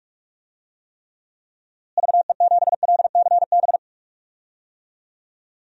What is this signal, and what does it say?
Morse code, heard as a keyed tone, about 46 wpm, 700 Hz. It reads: VE7LCB
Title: DXCC-100 Call Signs - 6 Characters - Repeated 3x - 45wpm